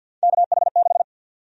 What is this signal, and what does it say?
Morse code, heard as a keyed tone, about 47 wpm, 700 Hz. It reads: KH6
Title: USA Call Sign Prefixes - Repeated 2x - 50wpm